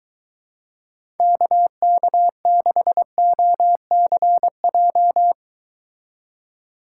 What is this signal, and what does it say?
Morse code, heard as a keyed tone, about 22 wpm, 700 Hz. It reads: KK6OCJ